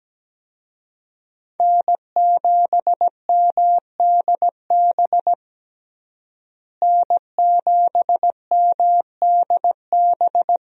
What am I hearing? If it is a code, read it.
Morse code, heard as a keyed tone, about 17 wpm, 700 Hz. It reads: N7MDB N7MDB
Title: USA Call Signs - 5 Characters - Repeated 3x - 17wpm